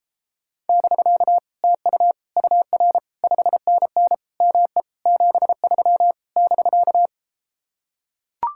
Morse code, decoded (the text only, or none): <BK> TU UR 5NN ME 73 <BK>